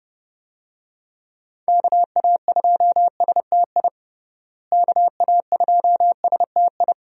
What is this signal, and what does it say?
Morse code, heard as a keyed tone, about 32 wpm, 700 Hz. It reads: KA2STI KA2STI
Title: USA Call Signs - 6 Characters - Repeated 2x - 30wpm